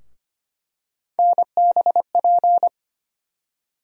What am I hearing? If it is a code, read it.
Morse code, heard as a keyed tone, about 25 wpm, 700 Hz. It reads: NBP
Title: USA Call Sign Postfixes - 3 Letters 25wpm